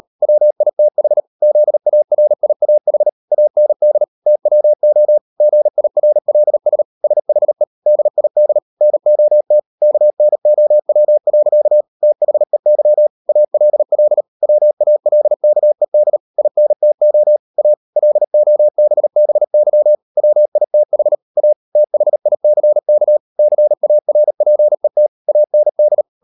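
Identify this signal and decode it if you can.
Morse code, heard as a keyed tone, about 38 wpm, 600 Hz. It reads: WITH ZARIAH AND TWO GIRLS SHE DID NOT KNOW. THEY ALL WALKED INTO A LOBBY WITH A THICK CARPET AND
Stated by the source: Morse code practice at 38 WPM, sending Ken Follett's novel Never in English, from CW med Gnister.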